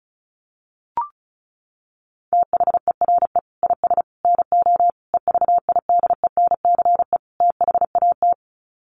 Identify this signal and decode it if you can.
Morse code, heard as a keyed tone, about 34 wpm, 700 Hz. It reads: THERE IS NO EVIDENCE THAT
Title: Sets of 5 Words Encore 35wpm